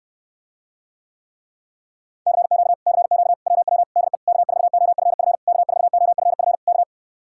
Morse code, heard as a keyed tone, about 69 wpm, 700 Hz. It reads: CQ CQ WW DE K4OFF K4OFF K